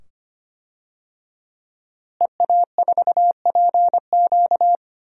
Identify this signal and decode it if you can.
Morse code, heard as a keyed tone, about 25 wpm, 700 Hz. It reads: EA4PQ
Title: DXCC-100 Call Signs - 5 Characters - Repeated 2x - 25wpm